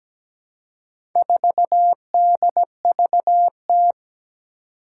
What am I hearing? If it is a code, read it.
Morse code, heard as a keyed tone, about 17 wpm, 700 Hz. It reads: 4DVT
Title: Four Challenging Letter-Number Combinations 17wpm